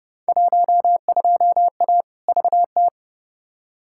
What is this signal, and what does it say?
Morse code, heard as a keyed tone, about 29 wpm, 700 Hz. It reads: 12A VT
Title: Field Day Exchanges Encore - 30wpm